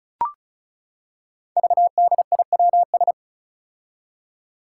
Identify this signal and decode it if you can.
Morse code, heard as a keyed tone, about 34 wpm, 700 Hz. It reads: VDIWS
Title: Five Challenging Letter-Number Combinations 35wpm